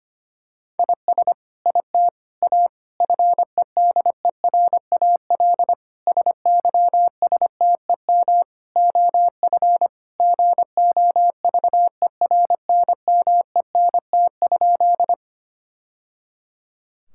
Morse code, heard as a keyed tone, about 25 wpm, 700 Hz. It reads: IS IT A FEDERAL SYSTEM OF GOVERNMENT?
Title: Sentences from Top 1000 Words - Review Part 5 - 25wpm